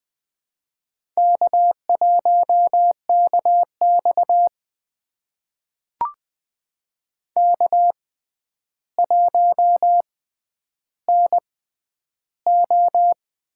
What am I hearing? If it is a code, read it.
Morse code, heard as a keyed tone, about 20 wpm, spaced as for 10 wpm, 700 Hz. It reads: K1KX K1NO
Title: USA Call Signs - 6x Character Spacing - 20wpm